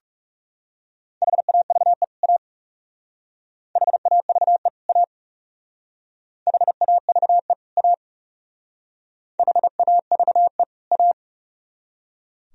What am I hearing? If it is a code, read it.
Morse code, heard as a keyed tone, about 38 wpm, 700 Hz. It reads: HAVE A HAVE A HAVE A HAVE A